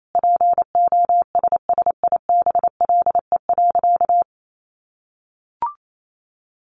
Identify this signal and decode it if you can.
Morse code, heard as a keyed tone, about 28 wpm, 700 Hz. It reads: POSSIBLE.
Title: Sentences from Top 900 Words 28wpm